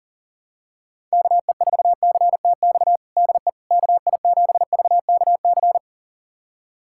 Morse code, heard as a keyed tone, about 37 wpm, 700 Hz. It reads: KE4CTX DE KI7VKC
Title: USA Callsign DE Callsign - 6 Characters - 40wpm